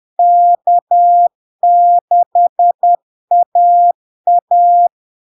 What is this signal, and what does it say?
Morse code, heard as a keyed tone, about 10 wpm, 700 Hz. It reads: K6AA
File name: LARK-FD-2022-Log-in-CW-10wpm-700-Hz